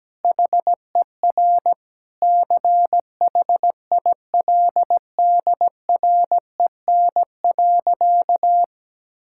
Morse code, heard as a keyed tone, about 17 wpm, 700 Hz. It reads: HER CHILDREN.
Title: Sentences from Top 1000 Words - Review Part 5 - 17wpm